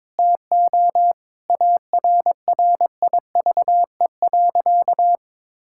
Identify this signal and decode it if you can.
Morse code, heard as a keyed tone, about 22 wpm, 700 Hz. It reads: TO ARRIVE.
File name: practice / 22wpm / Sentences with 3-Word Phrases - 22wpm